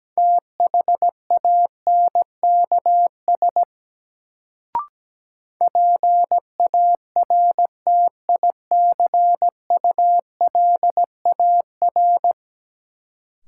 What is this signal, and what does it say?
Morse code, heard as a keyed tone, about 17 wpm, 700 Hz. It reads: THANKS PARTICULAR
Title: Sentences from Top 600 Words - New Word Drill - 17wpm